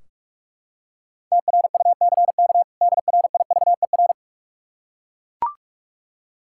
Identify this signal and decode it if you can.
Morse code, heard as a keyed tone, about 41 wpm, 700 Hz. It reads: TRUCK DRIVER
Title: Practice QSO - Exchange 3 - More Info - 45wpm